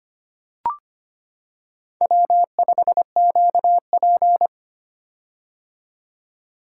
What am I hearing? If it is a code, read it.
Morse code, heard as a keyed tone, about 25 wpm, 700 Hz. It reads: W5QP
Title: K1USN SST QSO Practice - 25wpm